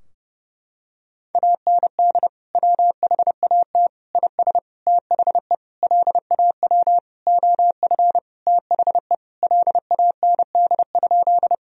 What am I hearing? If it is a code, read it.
Morse code, heard as a keyed tone, about 29 wpm, 700 Hz. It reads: AND WHAT IS THE LAW OF THE LAND?
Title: Sentences from Top 1000 Words - Review Part 3 - 30wpm